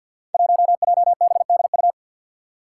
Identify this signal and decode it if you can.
Morse code, heard as a keyed tone, about 53 wpm, 700 Hz. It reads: 1JBDU
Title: Five Challenging Letter-Number Combinations 50wpm